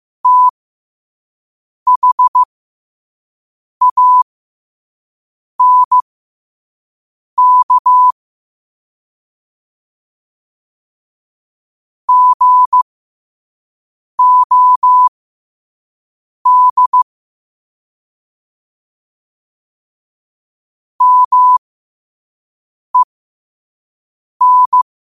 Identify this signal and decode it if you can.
Morse code, heard as a keyed tone, about 15 wpm, spaced as for 5 wpm, 1000 Hz. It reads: THANK GOD MEN
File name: qotdfiles / 5wpm